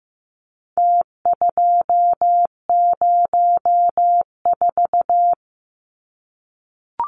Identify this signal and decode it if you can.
Morse code, heard as a keyed tone, about 15 wpm, 700 Hz. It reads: T204
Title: Numbers - Contest Style - 4-Digit 15wpm